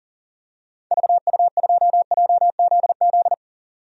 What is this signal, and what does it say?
Morse code, heard as a keyed tone, about 38 wpm, 700 Hz. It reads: VU2JZZ
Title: DXCC-100 Call Signs - 6 Characters - 40wpm